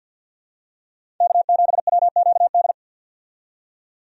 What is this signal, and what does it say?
Morse code, heard as a keyed tone, about 50 wpm, 700 Hz. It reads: K7WQD
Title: USA Call Signs - 5 Characters - 50wpm